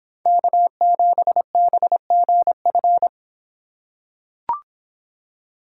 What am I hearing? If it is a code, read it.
Morse code, heard as a keyed tone, about 26 wpm, 700 Hz. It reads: K7BGF